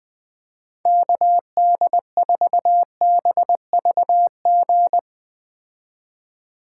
Morse code, heard as a keyed tone, about 21 wpm, 700 Hz. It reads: KD4BVG
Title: 2023 POTA Activator Call Signs - 20wpm